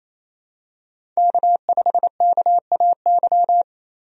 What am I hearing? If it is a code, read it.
Morse code, heard as a keyed tone, about 27 wpm, 700 Hz. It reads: K5KAY
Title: USA Call Signs - 5 Characters - 28wpm